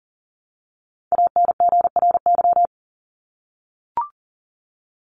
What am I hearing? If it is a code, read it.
Morse code, heard as a keyed tone, about 40 wpm, 700 Hz. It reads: ANGRY
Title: Top 1751-2000 Words 40wpm